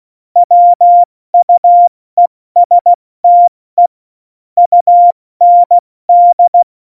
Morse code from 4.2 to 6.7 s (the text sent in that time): UND